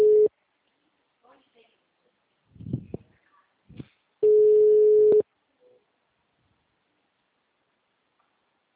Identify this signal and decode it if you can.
Ringback tone (Europe, 425 Hz), with speech.